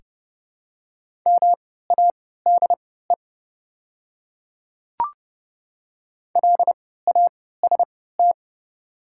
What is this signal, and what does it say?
Morse code, heard as a keyed tone, about 31 wpm, spaced as for 17 wpm, 700 Hz. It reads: MADE LAST
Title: 3x Character Spacing - Top 300 Words - 30wpm